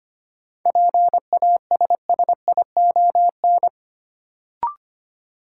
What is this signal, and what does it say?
Morse code, heard as a keyed tone, about 25 wpm, 700 Hz. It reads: PASSION
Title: Sentences from Top 1300 Words - New Word Drill - 25wpm